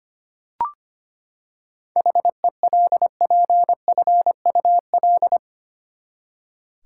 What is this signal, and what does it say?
Morse code, heard as a keyed tone, about 25 wpm, 700 Hz. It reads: HELPFUL
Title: Seven Letter Words - 25wpm